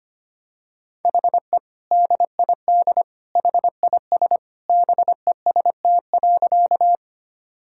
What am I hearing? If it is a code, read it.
Morse code, heard as a keyed tone, about 25 wpm, 700 Hz. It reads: HE DID HIS BEST.
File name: practice / 25wpm / Sentences with 2-Word Phrases - 25wpm